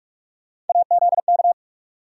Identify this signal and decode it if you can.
Morse code, heard as a keyed tone, about 44 wpm, 700 Hz. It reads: AZK
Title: USA Call Sign Postfixes - 3 Letters 45wpm